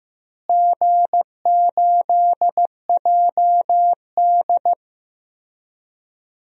Morse code, heard as a keyed tone, about 15 wpm, 700 Hz. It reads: G8JD
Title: DXCC-100 Call Signs - 4 Characters - Repeated 2x - 15wpm